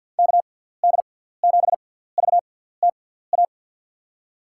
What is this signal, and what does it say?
Morse code, heard as a keyed tone, about 51 wpm, spaced as for 17 wpm, 700 Hz. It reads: KD7VTA
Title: USA Call Signs - 6 Characters - 6x Character Spacing - 50wpm